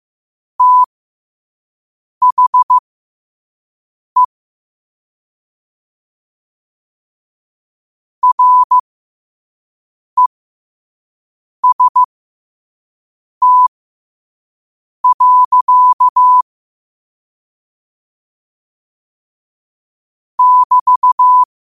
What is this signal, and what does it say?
Morse code, heard as a keyed tone, about 15 wpm, spaced as for 5 wpm, 1000 Hz. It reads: THE REST. <BT>
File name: qotdfiles / 5wpm